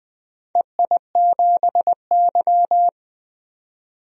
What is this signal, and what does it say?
Morse code, heard as a keyed tone, about 20 wpm, 700 Hz. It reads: EI7Y
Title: DXCC-100 Call Signs - 4 Characters - 20wpm